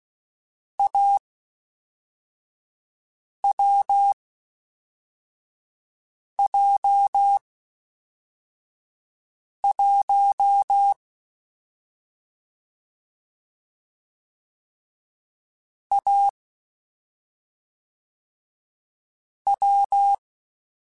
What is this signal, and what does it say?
Morse code, heard as a keyed tone, about 16 wpm, spaced as for 4 wpm, 800 Hz. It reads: AWJ1 A W
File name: Difficult Combos / Difficult Combos-mp3s / AWJ1 @5